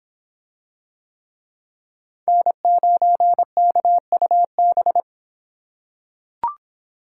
Morse code, heard as a keyed tone, about 25 wpm, 700 Hz. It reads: N9KUB